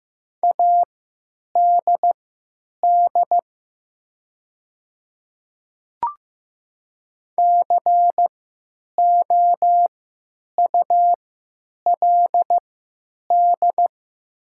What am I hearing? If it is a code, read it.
Morse code, heard as a keyed tone, about 15 wpm, spaced as for 9 wpm, 700 Hz. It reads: ADD COULD
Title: 3x Character Spacing - Top 200 Words - 15wpm